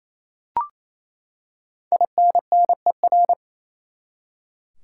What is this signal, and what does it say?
Morse code, heard as a keyed tone, about 28 wpm, 700 Hz. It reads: INNER
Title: Five Letter Words - 28wpm